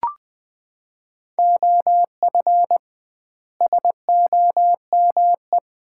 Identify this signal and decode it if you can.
Morse code, heard as a keyed tone, about 20 wpm, 700 Hz. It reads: OF SOME
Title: 2x Word-Spacing - Sets of 4 Words - 20wpm